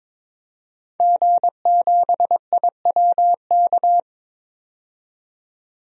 Morse code, heard as a keyed tone, about 21 wpm, 700 Hz. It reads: G7IWK
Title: DXCC-100 Call Signs - 5 Characters - 22wpm